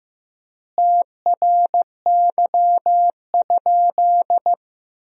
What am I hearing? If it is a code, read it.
Morse code, heard as a keyed tone, about 15 wpm, 700 Hz. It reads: TRY?
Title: 4x Word-Spacing - Sentences from Top 200 Words - 15wpm